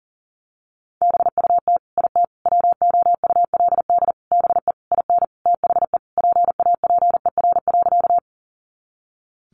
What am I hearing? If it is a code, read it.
Morse code, heard as a keyed tone, about 40 wpm, 700 Hz. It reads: BUT IT WOULD BE IN THE PAPER.
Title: Sentences from Top 300 Words 40wpm